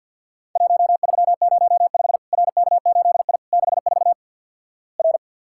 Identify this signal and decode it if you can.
Morse code, heard as a keyed tone, about 51 wpm, 700 Hz. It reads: 1305 RK8I 64 R
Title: Worked All Europe DX Contest - QTC Exchange - 50wpm